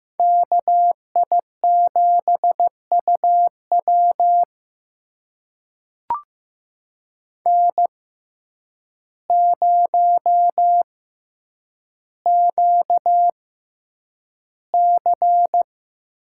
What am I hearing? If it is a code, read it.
Morse code, heard as a keyed tone, about 15 wpm, spaced as for 8 wpm, 700 Hz. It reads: KI7UW N0QC
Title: USA Call Signs - 5 Characters - 6x Character Spacing - 15wpm